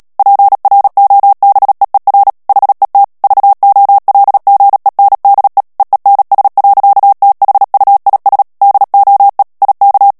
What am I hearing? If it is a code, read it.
Morse code, heard as a keyed tone, about 37 wpm, 800 Hz. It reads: PROBEER HET VOLGENDE EENS. THUIS DOE IK
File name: morse mp3 / HOLLANDAIS texte clair 2 vitesse 35 mots minute